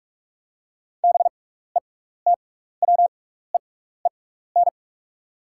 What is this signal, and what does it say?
Morse code, heard as a keyed tone, about 44 wpm, spaced as for 15 wpm, 690 Hz. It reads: BETWEEN
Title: 6x Character Spacing - Top 300 Words - 45wpm